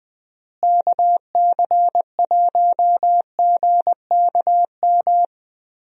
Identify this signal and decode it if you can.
Morse code, heard as a keyed tone, about 20 wpm, 700 Hz. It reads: KC1GKM
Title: USA Call Signs - 6 Characters - Repeated 2x - 20wpm